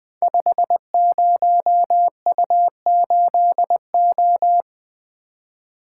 Morse code, heard as a keyed tone, about 20 wpm, 700 Hz. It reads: 50U8O
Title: Code Groups - 5 Letter:Numbers - 20wpm